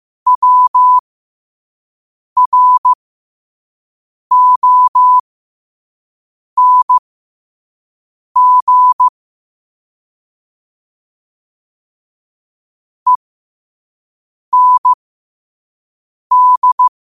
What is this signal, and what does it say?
Morse code, heard as a keyed tone, about 15 wpm, spaced as for 5 wpm, 1000 Hz. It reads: WRONG END
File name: qotdfiles / 5wpm